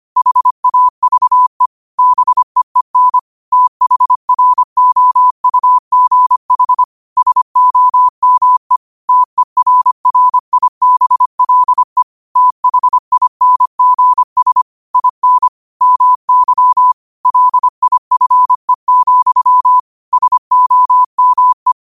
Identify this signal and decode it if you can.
Morse code, heard as a keyed tone, about 25 wpm, 1000 Hz. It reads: HAVE BEEN THROUGH SOME TERRIBLE THINGS IN MY LIFE, SOME
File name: qotdfiles / 25wpm